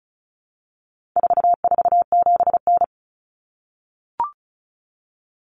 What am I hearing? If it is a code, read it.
Morse code, heard as a keyed tone, about 34 wpm, 700 Hz. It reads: 447N
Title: Common Cut Numbers - 4-Digit 35wpm